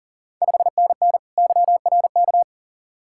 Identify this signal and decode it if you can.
Morse code, heard as a keyed tone, about 40 wpm, 690 Hz. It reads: 5NN YRK
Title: Ontario QSO Party Exchange 40wpm